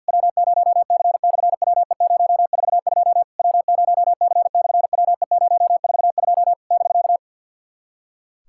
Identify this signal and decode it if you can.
Morse code, heard as a keyed tone, about 51 wpm, 690 Hz. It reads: W0C/WE042 W0C/WE042 <BK>
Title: SOTA QSO Practice 50wpm